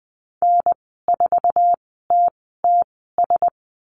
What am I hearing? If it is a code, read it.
Morse code, heard as a keyed tone, about 20 wpm, spaced as for 14 wpm, 700 Hz. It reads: N4TTS
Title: 048 Lesson 9g - Number 4 - Call Signs 20wpm